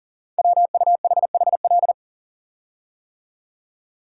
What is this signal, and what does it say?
Morse code, heard as a keyed tone, about 44 wpm, 700 Hz. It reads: WUHHL
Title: Code Groups - 5 Letters - 40wpm